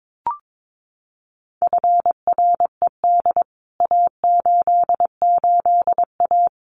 Morse code, heard as a keyed tone, about 22 wpm, 700 Hz. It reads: FRED A88A
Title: CWT Exchange 22wpm